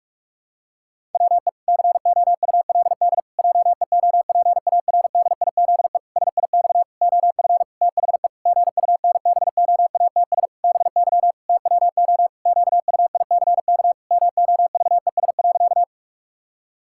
Morse code, heard as a keyed tone, about 46 wpm, 700 Hz. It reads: WE COULD JEOPARDIZE SIX OF THE GUNBOATS BY TWO QUICK MOVES.
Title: Pangram Challenge v2 - 45wpm